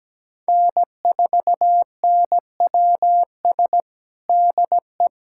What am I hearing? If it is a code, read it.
Morse code, heard as a keyed tone, about 17 wpm, 700 Hz. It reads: N4NWS DE
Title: USA Callsign DE Callsign - 5 Characters - 17wpm